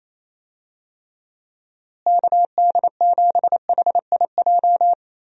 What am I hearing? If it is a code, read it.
Morse code, heard as a keyed tone, about 28 wpm, 700 Hz. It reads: KD7HIJ
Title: USA Call Signs - 6 Characters - Repeated 2x - 28wpm